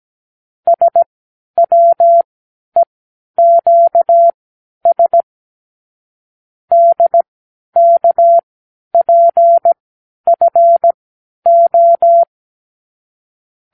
Morse code, heard as a keyed tone, about 17 wpm, spaced as for 10 wpm, 680 Hz. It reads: SWEQS DKPFO
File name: k7qo / Lesson 36 (A-X)